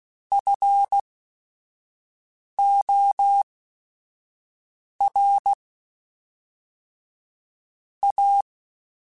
Morse code, heard as a keyed tone, about 16 wpm, spaced as for 5 wpm, 800 Hz. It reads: FOR A